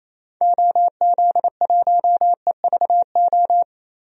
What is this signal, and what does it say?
Morse code, heard as a keyed tone, about 28 wpm, 700 Hz. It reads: OZ1EVO
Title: DXCC-100 Call Signs - 6 Characters - Repeated 2x - 28wpm